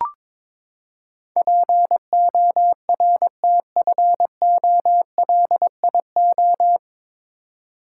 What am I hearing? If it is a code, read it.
Morse code, heard as a keyed tone, about 21 wpm, 700 Hz. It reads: PORTFOLIO
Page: Morse Code Ninja practice 22wpm